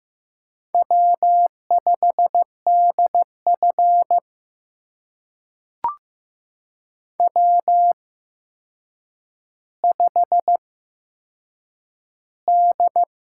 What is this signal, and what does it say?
Morse code, heard as a keyed tone, about 15 wpm, spaced as for 7 wpm, 700 Hz. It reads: W5DF W5D
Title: USA Call Signs - 8x Character Spacing - 15wpm